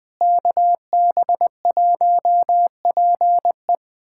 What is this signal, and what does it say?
Morse code, heard as a keyed tone, about 21 wpm, 700 Hz. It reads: KB1PE
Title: USA Callsign DE Callsign 20wpm